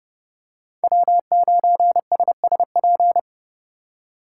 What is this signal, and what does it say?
Morse code, heard as a keyed tone, about 30 wpm, 700 Hz. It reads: W9SSP